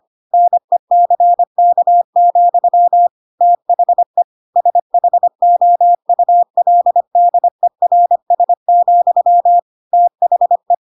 Morse code, heard as a keyed tone, about 25 wpm, 700 Hz. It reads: NECK, THE SHOULDERS, THE
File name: Pinocchio / Pinocchio ch3 25 25 ik1hge